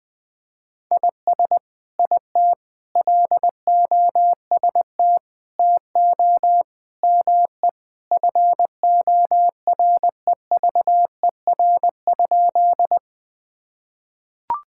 Morse code, heard as a keyed tone, about 20 wpm, 700 Hz. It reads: IS IT LOST TO ME FOREVER?
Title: Sentences from Top 1600 Words 20wpm